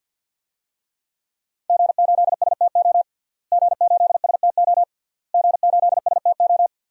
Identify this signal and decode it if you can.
Morse code, heard as a keyed tone, about 49 wpm, 700 Hz. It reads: G8STO G8STO G8STO
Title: DXCC-100 Call Signs - 5 Characters - Repeated 3x - 50wpm